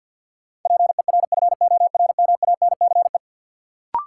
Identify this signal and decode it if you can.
Morse code, heard as a keyed tone, about 48 wpm, 700 Hz. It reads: PERFORMANCE
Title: Sentences from Top 1000 Words - New Word Drill - 50wpm